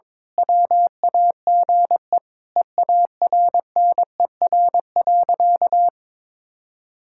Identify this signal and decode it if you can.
Morse code, heard as a keyed tone, about 22 wpm, 700 Hz. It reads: WAGE EARNER.
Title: Pangram Challenge v2 - 22wpm